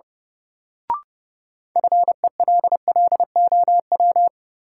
Morse code, heard as a keyed tone, about 32 wpm, 700 Hz. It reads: FELLOW
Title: Sets of 5 Words Encore - New Word Drill - 30wpm